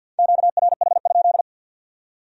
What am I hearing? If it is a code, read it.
Morse code, heard as a keyed tone, about 64 wpm, 700 Hz. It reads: QRH?